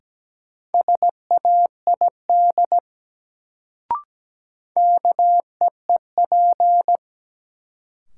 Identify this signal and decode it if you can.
Morse code, heard as a keyed tone, about 17 wpm, 700 Hz. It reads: SAID KEEP
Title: Sets of 2 Words - New Word Drill - 17wpm